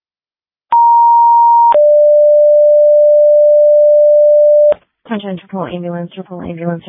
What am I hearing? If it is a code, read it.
Two-tone page, heard as a one-second tone, then a three-second tone, about 940 then 590 Hz, with speech. It is heard from 0.7 to 4.7 s.